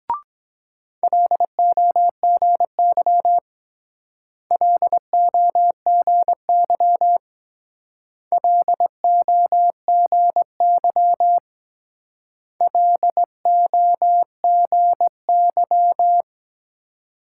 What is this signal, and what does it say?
Morse code, heard as a keyed tone, about 20 wpm, 700 Hz. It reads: LOGY LOGY LOGY LOGY
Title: Speed-Racing Multi-letter Phonemes - 17wpm